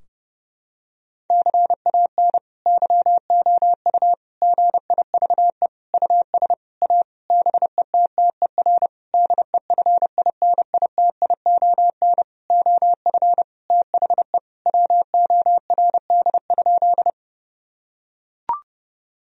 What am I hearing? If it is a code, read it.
Morse code, heard as a keyed tone, about 30 wpm, 700 Hz. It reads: CAN YOU GIVE US A BETTER DEFINITION OF THE WORD?
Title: Sentences from Top 1000 Words - Review Part 4 - 30wpm